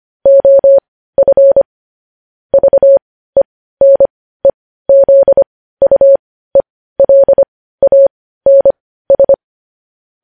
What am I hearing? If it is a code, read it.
Morse code, heard as a keyed tone, about 25 wpm, spaced as for 15 wpm, 550 Hz. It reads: OF VENEZUELANS
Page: news headlines in morse code at 15 wpm